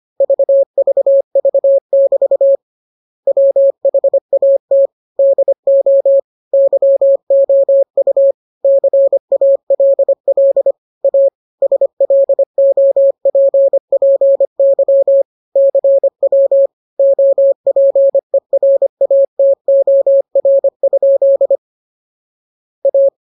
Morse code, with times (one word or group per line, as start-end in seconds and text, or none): 0.2-2.6 VVV<BT>
3.3-4.9 WHAT
5.2-6.2 DO
6.5-8.3 YOU
8.6-10.7 CALL
11.0-11.3 A
11.6-15.2 SLOPPY
15.6-16.7 CW
17.0-21.6 OPERATOR?
22.9-23.1 A